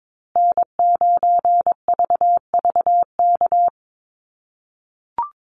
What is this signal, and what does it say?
Morse code, heard as a keyed tone, about 22 wpm, 700 Hz. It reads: N9VVK
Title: USA Call Signs Repeated 2x 22wpm